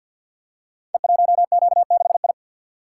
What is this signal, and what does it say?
Morse code, heard as a keyed tone, about 52 wpm, 700 Hz. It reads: E1Q6I
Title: Code Groups - 5 Letter:Numbers - 50wpm